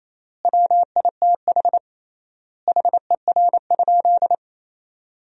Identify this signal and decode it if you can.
Morse code, heard as a keyed tone, about 28 wpm, 700 Hz. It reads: WITH HER?
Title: 3x Word-Spacing - Sentences from Top 400 Words - 28wpm